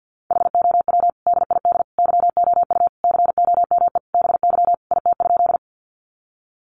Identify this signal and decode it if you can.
Morse code, heard as a keyed tone, about 50 wpm, 700 Hz. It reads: HOW DID YOU COME BY IT?
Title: Sentences from Top 100 Words 50wpm